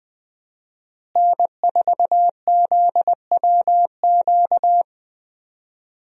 Morse code, heard as a keyed tone, about 20 wpm, 700 Hz. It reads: N4ZWQ
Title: USA Call Signs - 5 Characters - Repeated 3x - 20wpm